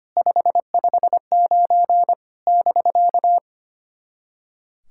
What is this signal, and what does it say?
Morse code, heard as a keyed tone, about 25 wpm, 660 Hz. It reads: 559 <BK>
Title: SOTA QSO Practice 25wpm